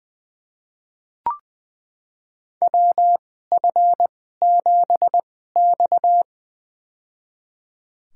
Morse code, heard as a keyed tone, about 20 wpm, spaced as for 14 wpm, 700 Hz. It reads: WF7X
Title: USA Call Signs - 4 Characters - 2x Character Spacing - 20wpm